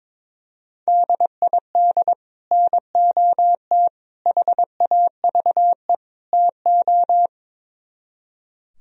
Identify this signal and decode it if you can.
Morse code, heard as a keyed tone, about 22 wpm, 700 Hz. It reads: DID NOT HAVE TO
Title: Speed-Racing - Sets of 4 Words 22wpm